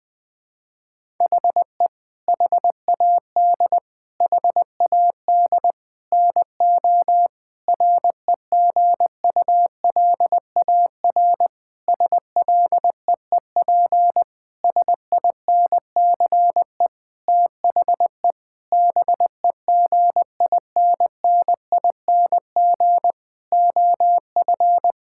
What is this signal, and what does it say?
Morse code, heard as a keyed tone, about 21 wpm, 700 Hz. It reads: HE HAD HAD NO REGULAR SLEEP SINCE THE BEGINNING OF
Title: Sentences from Top 1000 Words - Review Part 6 - 20wpm